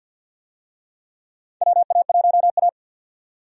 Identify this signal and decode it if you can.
Morse code, heard as a keyed tone, about 52 wpm, 700 Hz. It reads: WA1A
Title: USA Call Signs - 4 Characters - 50wpm